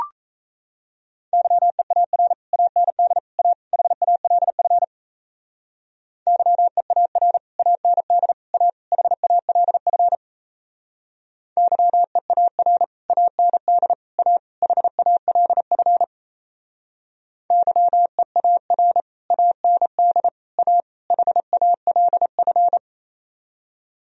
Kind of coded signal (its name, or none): Morse code